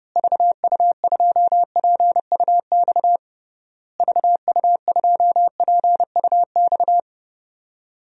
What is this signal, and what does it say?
Morse code, heard as a keyed tone, about 30 wpm, 700 Hz. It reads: VU2PUX VU2PUX
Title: DXCC-100 Call Signs - 6 Characters - Repeated 3x - 30wpm